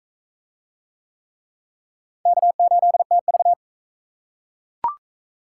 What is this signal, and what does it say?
Morse code, heard as a keyed tone, about 41 wpm, 700 Hz. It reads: K8TV